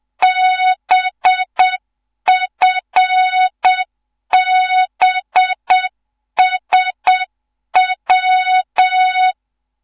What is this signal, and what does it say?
Morse code, heard as a keyed tone, about 7 wpm, 3700 Hz, with noise. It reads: BFBSW
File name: Morze-HA2MN / hang / 14 280 HA2MN